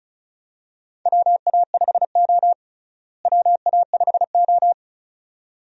Morse code, heard as a keyed tone, about 36 wpm, 700 Hz. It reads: WA5O WA5O
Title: USA Call Signs - 4 Characters - Repeated 2x - 35wpm